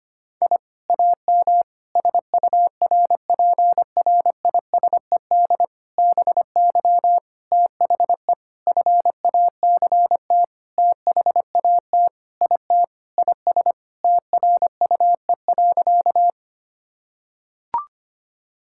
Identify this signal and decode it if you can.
Morse code, heard as a keyed tone, about 25 wpm, 700 Hz. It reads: I AM SURPRISED BY THE FACT THAT IT IS TRUE.
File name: practice / 25wpm / Sentences with 4-Word Phrases - 25wpm